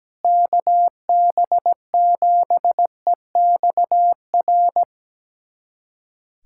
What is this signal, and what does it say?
Morse code, heard as a keyed tone, about 17 wpm, 700 Hz. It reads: KB7EXR
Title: USA Callsign DE Callsign 17wpm